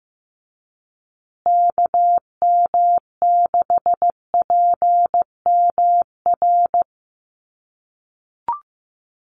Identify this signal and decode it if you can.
Morse code, heard as a keyed tone, about 15 wpm, 700 Hz. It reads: KM6PMR
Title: USA Call Signs Repeated 2x 15wpm